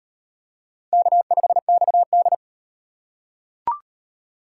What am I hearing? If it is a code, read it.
Morse code, heard as a keyed tone, about 34 wpm, 700 Hz. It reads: K5XD